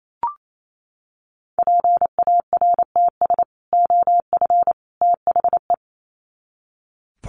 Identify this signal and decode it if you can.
Morse code, heard as a keyed tone, about 28 wpm, 700 Hz. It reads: PARTS OF THE
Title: Sets of 3 Words 28wpm